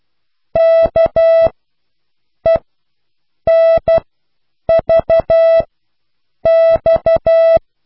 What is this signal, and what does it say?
Morse code, heard as a keyed tone, about 12 wpm, spaced as for 7 wpm, 660 Hz. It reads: KENVX